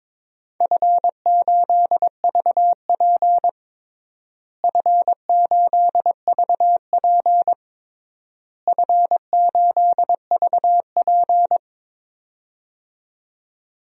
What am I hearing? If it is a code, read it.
Morse code, heard as a keyed tone, about 22 wpm, 700 Hz. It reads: F8VP F8VP F8VP
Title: DXCC-100 Call Signs - 4 Characters - Repeated 3x - 22wpm